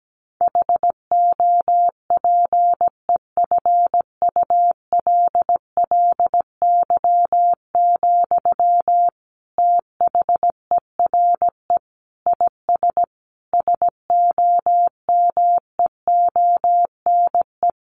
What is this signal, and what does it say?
Morse code, heard as a keyed tone, about 17 wpm, 700 Hz. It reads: HOPEFULLY, THERE IS SOMEONE
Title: Sentences from Top 1500 Words 17wpm